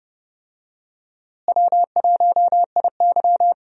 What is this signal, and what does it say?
Morse code, heard as a keyed tone, about 30 wpm, 700 Hz. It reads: W1IY
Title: USA Call Signs - 4 Characters - Repeated 2x - 30wpm